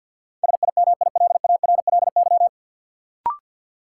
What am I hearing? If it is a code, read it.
Morse code, heard as a keyed tone, about 53 wpm, 700 Hz. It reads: SIMILARLY
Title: Sentences from Top 1400 Words - New Word Drill Alt - 50wpm